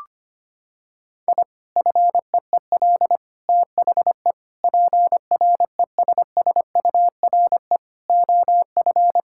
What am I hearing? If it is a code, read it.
Morse code, heard as a keyed tone, about 25 wpm, 700 Hz. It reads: I FEEL THE PRESSURE OF
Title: Sentences from Top 1000 Words - Review Part 6 - 25wpm